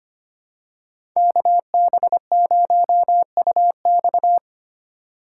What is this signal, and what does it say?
Morse code, heard as a keyed tone, about 25 wpm, 700 Hz. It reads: KB0UX
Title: USA Call Signs - 5 Characters - 4x Character Spacing - 25wpm